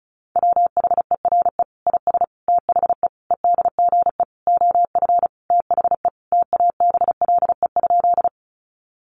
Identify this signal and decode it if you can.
Morse code, heard as a keyed tone, about 34 wpm, 700 Hz. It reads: WHERE IS THE EDGE OF THE TABLE?
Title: Sentences from Top 1300 Words 35wpm